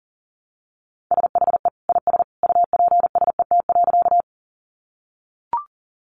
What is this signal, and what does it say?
Morse code, heard as a keyed tone, about 38 wpm, 700 Hz. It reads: SHE IS UPSET.
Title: Sentences from Top 800 Words 40wpm